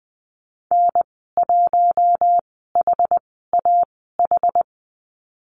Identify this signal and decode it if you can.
Morse code, heard as a keyed tone, about 20 wpm, spaced as for 14 wpm, 700 Hz. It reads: N1HAH
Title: 193 Lesson 31g - Letter v - Call Signs 20wpm